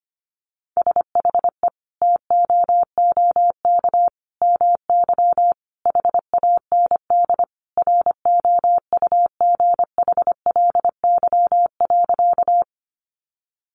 Morse code, heard as a keyed tone, about 25 wpm, 700 Hz. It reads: SHE TOOK MY HAND ROUGHLY.
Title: Sentences from Top 1500 Words 25wpm